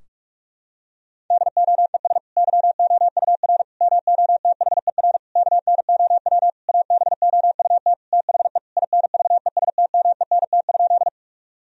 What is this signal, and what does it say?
Morse code, heard as a keyed tone, about 41 wpm, 700 Hz. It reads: DOES YOUR MOTHER KNOW ABOUT THE INVESTMENT?
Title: Sentences from Top 1000 Words - Review Part 4 - 45wpm